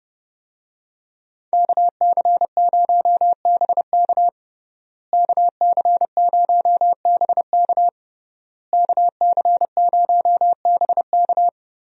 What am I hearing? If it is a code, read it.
Morse code, heard as a keyed tone, about 30 wpm, 700 Hz. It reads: KC0BK KC0BK KC0BK
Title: USA Call Signs - 5 Characters - Repeated 3x - 30wpm